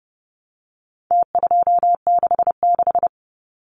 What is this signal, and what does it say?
Morse code, heard as a keyed tone, about 30 wpm, 700 Hz. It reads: T266
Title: Numbers - Contest Style - 4-Digit 30wpm